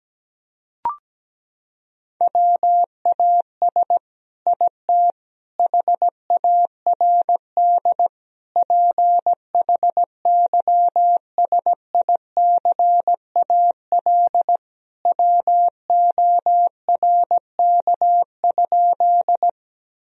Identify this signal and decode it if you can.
Morse code, heard as a keyed tone, about 17 wpm, 700 Hz. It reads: WAS IT HARD PHYSICAL WORK?
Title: Sentences from Top 1000 Words - Review Part 4 - 17wpm